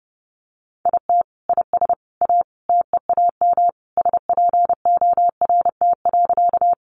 Morse code, heard as a keyed tone, about 30 wpm, 700 Hz. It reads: IT IS A TEAM SPORT.
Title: Sentences from Top 1500 Words 30wpm